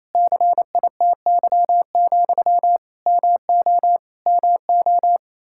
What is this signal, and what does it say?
Morse code, heard as a keyed tone, about 28 wpm, 700 Hz. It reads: CITY, MO MO
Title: Practice QSO - Exchange 1 - Beginning - 28wpm